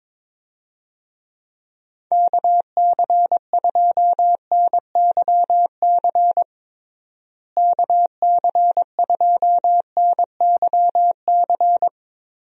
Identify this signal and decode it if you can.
Morse code, heard as a keyed tone, about 23 wpm, 700 Hz. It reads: KC2NYC KC2NYC
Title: USA Call Signs - 6 Characters - Repeated 3x - 22wpm